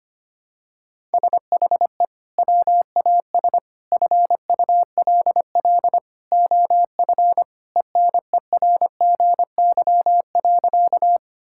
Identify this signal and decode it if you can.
Morse code, heard as a keyed tone, about 25 wpm, 700 Hz. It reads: SHE WAS FULL OF ENERGY.
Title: Sentences from Top 1000 Words - Review Part 6 - 25wpm